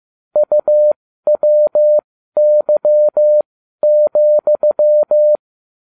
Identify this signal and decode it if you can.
Morse code, heard as a keyed tone, about 15 wpm, spaced as for 12 wpm, 600 Hz. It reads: UWY,